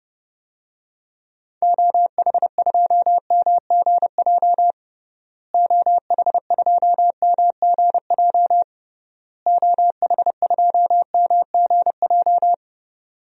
Morse code, heard as a keyed tone, about 30 wpm, 700 Hz. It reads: OH2MGJ OH2MGJ OH2MGJ
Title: DXCC-100 Call Signs - 6 Characters - Repeated 3x - 30wpm